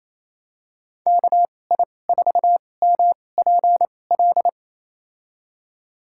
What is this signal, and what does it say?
Morse code, heard as a keyed tone, about 27 wpm, spaced as for 20 wpm, 700 Hz. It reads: KI4MPL
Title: USA Call Signs - 6 Characters - 2x Character Spacing - 28wpm